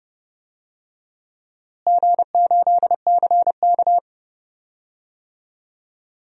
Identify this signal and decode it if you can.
Morse code, heard as a keyed tone, about 30 wpm, 700 Hz. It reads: G8CK
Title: DXCC-100 Call Signs - 4 Characters - Repeated 3x - 30wpm